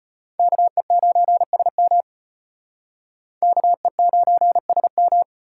Morse code, heard as a keyed tone, about 38 wpm, 700 Hz. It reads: KE9SM KE9SM